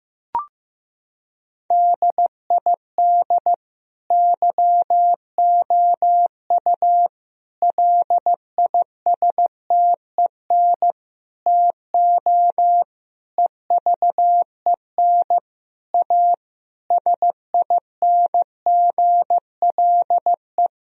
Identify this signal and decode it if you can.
Morse code, heard as a keyed tone, about 15 wpm, 700 Hz. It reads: DID YOU LISTEN TO EVEN A SINGLE
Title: Sentences from Top 1000 Words - Review Part 4 - 15wpm